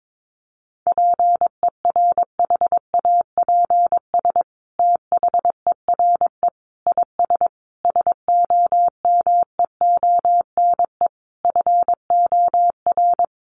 Morse code, heard as a keyed tone, about 22 wpm, 700 Hz. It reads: PERHAPS THERE IS SOMEONE FOR
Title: Sentences from Top 600 Words 22wpm